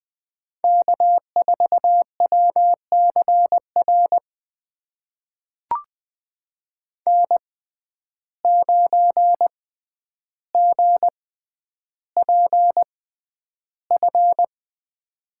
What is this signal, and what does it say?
Morse code, heard as a keyed tone, about 20 wpm, spaced as for 10 wpm, 700 Hz. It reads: K4WCR N9GPF
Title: USA Call Signs - 5 Characters - 6x Character Spacing - 20wpm